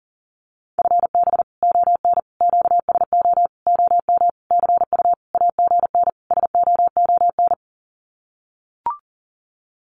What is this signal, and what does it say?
Morse code, heard as a keyed tone, about 40 wpm, 700 Hz. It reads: FB ON QSO OM CU AGN SOON
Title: Common QSO Phrases 40wpm